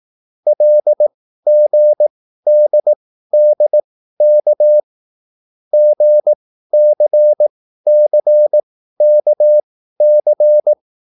Morse code, heard as a keyed tone, about 18 wpm, spaced as for 13 wpm, 600 Hz. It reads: LGDDK GCCKC